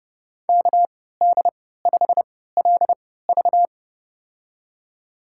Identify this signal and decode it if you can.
Morse code, heard as a keyed tone, about 30 wpm, spaced as for 17 wpm, 700 Hz. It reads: KD5LV
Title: USA Call Signs - 5 Characters - 3x Character Spacing - 30wpm